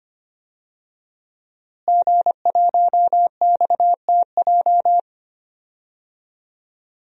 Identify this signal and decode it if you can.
Morse code, heard as a keyed tone, about 25 wpm, 700 Hz. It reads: G1XTJ
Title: DXCC-100 Call Signs - 5 Characters - Repeated 3x - 25wpm